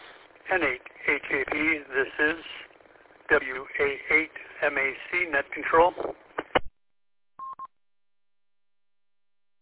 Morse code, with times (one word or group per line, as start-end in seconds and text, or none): none